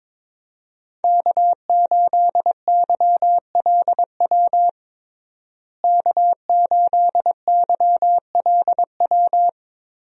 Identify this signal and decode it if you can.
Morse code, heard as a keyed tone, about 22 wpm, 700 Hz. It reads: K8YLW K8YLW
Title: USA Call Signs - 5 Characters - Repeated 3x - 22wpm